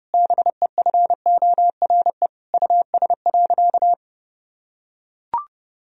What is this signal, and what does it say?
Morse code, heard as a keyed tone, about 32 wpm, 700 Hz. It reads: BEFORE US.
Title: Sentences from Top 1000 Words - Review Part 4 - 30wpm